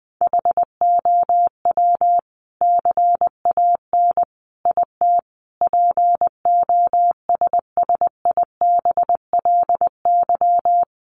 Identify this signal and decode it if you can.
Morse code, heard as a keyed tone, about 20 wpm, 700 Hz. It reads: HOW CAN IT POSSIBLY
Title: Sentences from Top 1300 Words 20wpm